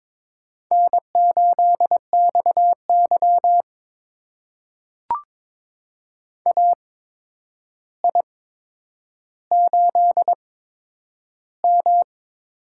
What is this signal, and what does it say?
Morse code, heard as a keyed tone, about 22 wpm, spaced as for 9 wpm, 700 Hz. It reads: N8XY AI8M
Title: USA Call Signs - 4 Characters - 8x Character Spacing - 22wpm